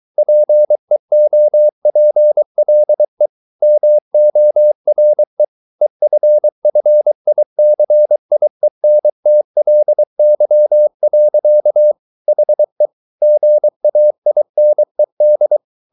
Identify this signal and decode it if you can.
Morse code, heard as a keyed tone, about 23 wpm, 600 Hz. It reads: PEOPLE MORE EFFICIENTLY. HE GAINED